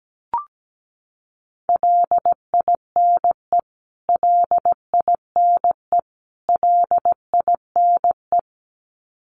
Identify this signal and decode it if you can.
Morse code, heard as a keyed tone, about 17 wpm, 700 Hz. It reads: LINE LINE LINE
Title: Top 500 Words Repeated 17wpm